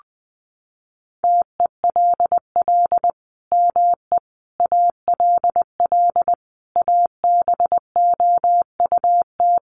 Morse code, heard as a keyed tone, about 19 wpm, 700 Hz. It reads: TELL ME ALL ABOUT
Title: Sentences from Top 200 Words 20wpm